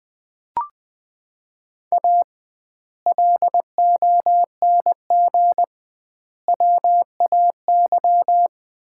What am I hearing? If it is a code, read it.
Morse code, heard as a keyed tone, about 21 wpm, 700 Hz. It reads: A LONG WAY
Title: 2x Word-Spacing - Sets of 5 Words - 20wpm